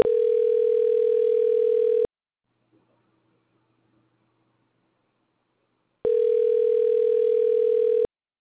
Ringback tone (North America), 440 + 480 Hz.